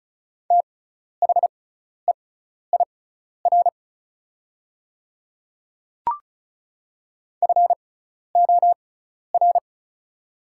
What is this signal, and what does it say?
Morse code, heard as a keyed tone, about 36 wpm, spaced as for 12 wpm, 700 Hz. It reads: THEIR FOR
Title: 6x Character Spacing - Top 100 Words - 35wpm